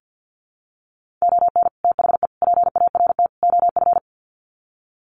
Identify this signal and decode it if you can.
Morse code, heard as a keyed tone, about 50 wpm, 700 Hz. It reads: ON THE PART OF
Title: Sets of 4 Words 50wpm